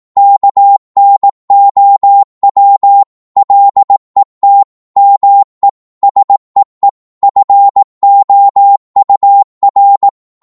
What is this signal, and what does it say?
Morse code, heard as a keyed tone, about 18 wpm, 800 Hz. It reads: KNOW LET ME SEE FOUR